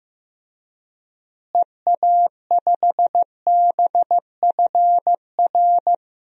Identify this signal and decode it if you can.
Morse code, heard as a keyed tone, about 15 wpm, 700 Hz. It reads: EA5BFR
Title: DXCC-100 Call Signs - 6 Characters - Repeated 3x - 15wpm